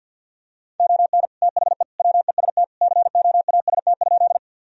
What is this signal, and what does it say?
Morse code, heard as a keyed tone, about 50 wpm, 700 Hz. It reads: ON THE WEST COAST?
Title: Sentences from Top 1000 Words - Review Part 4 - 50wpm